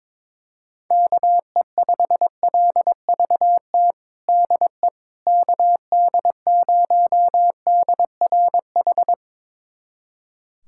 Morse code, heard as a keyed tone, about 22 wpm, 700 Hz. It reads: KE5LVT DE KD0DRH
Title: USA Callsign DE Callsign - 6 Characters - 22wpm